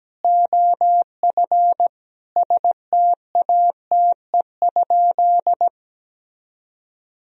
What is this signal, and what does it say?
Morse code, heard as a keyed tone, about 17 wpm, 700 Hz. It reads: OF STATE?
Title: Sentences from Top 1000 Words - Review Part 2 - 17wpm